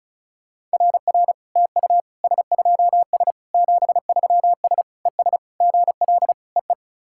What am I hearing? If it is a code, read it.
Morse code, heard as a keyed tone, about 36 wpm, 690 Hz. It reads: RR TU S2S 73S ES GL EE
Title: SOTA QSO Practice 35wpm